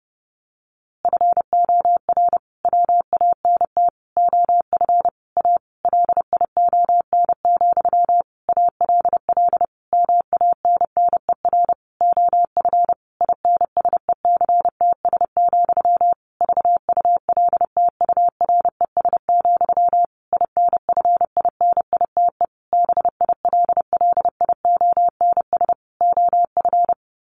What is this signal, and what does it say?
Morse code, heard as a keyed tone, about 30 wpm, 700 Hz. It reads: FOR WANT OF A LION, ALL MANNER OF INSECTS, VULTURES, INFINITE BILLIONS OF
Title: A Sound Of Thunder 30wpm